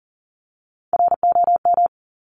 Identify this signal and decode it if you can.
Morse code, heard as a keyed tone, about 39 wpm, 700 Hz. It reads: ROM
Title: Top Three Letter Combinations in Words 40wpm